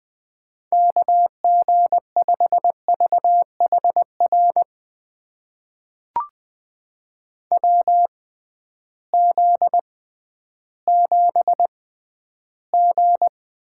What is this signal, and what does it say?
Morse code, heard as a keyed tone, about 20 wpm, spaced as for 12 wpm, 700 Hz. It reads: KG5VHR WZ7G
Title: USA Call Signs - 6x Character Spacing - 20wpm